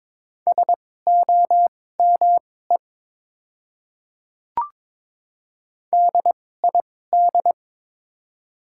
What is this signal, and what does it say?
Morse code, heard as a keyed tone, about 22 wpm, spaced as for 16 wpm, 700 Hz. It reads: SOME DID
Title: 2x Character Spacing - Top 100 Words - 22wpm